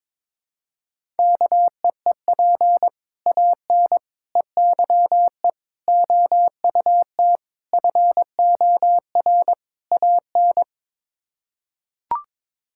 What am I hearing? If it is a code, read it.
Morse code, heard as a keyed tone, about 22 wpm, 700 Hz. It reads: KEEP AN EYE OUT FOR AN
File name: practice / 22wpm / Sets of 6 Words - 22wpm